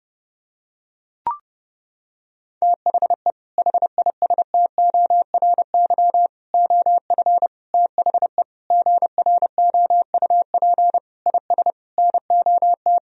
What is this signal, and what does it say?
Morse code, heard as a keyed tone, about 30 wpm, 700 Hz. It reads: THE HISTORY OF THE GROUP IS NOT
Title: Sentences from Top 1000 Words - Review Part 1 - 30wpm